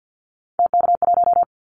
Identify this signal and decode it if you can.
Morse code, heard as a keyed tone, about 50 wpm, 700 Hz. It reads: TK1
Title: Callsign Prefixes for Top 100 DXCC - 50wpm